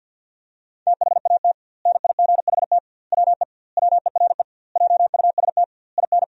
Morse code, heard as a keyed tone, about 52 wpm, 700 Hz. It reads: THAT NIGHT WE WERE JUST IN
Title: 2x Word-Spacing - Sentences from Top 300 Words - 50wpm